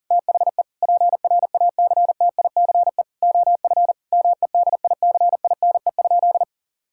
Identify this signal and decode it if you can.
Morse code, heard as a keyed tone, about 40 wpm, 700 Hz. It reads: THE PRACTICE OF MEDICINE?
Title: Sentences from Top 1000 Words - Review Part 6 - 40wpm